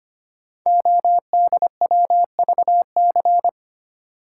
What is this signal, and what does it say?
Morse code, heard as a keyed tone, about 25 wpm, 700 Hz. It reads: ODWVC
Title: Code Groups - 5 Letters - 25wpm